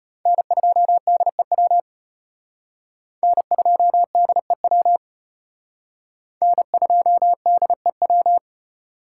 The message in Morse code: N2DEW N2DEW N2DEW

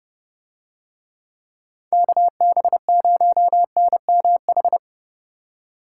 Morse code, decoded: KB0NMH

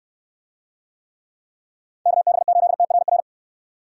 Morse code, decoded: KB8TRR

